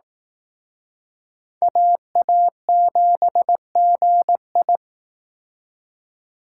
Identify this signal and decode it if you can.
Morse code, heard as a keyed tone, about 18 wpm, 700 Hz. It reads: AA7GI
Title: Speed Racing - USA Call Signs 15wpm